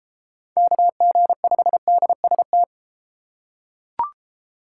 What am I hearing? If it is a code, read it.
Morse code, heard as a keyed tone, about 32 wpm, 700 Hz. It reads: KG5DST